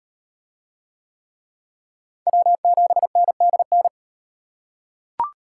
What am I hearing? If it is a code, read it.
Morse code, heard as a keyed tone, about 38 wpm, 700 Hz. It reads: W7NDN